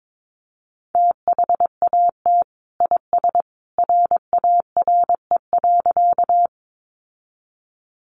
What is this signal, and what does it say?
Morse code, heard as a keyed tone, about 22 wpm, 700 Hz. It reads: THAT IS RARE.
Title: Sentences from Top 1100 Words 22wpm